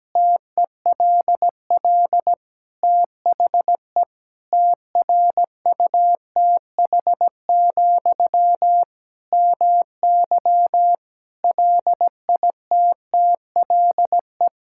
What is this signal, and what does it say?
Morse code, heard as a keyed tone, about 17 wpm, 700 Hz. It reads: TELL THE TRUTH, MY LITTLE
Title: Sentences from Top 1000 Words - Review Part 6 - 17wpm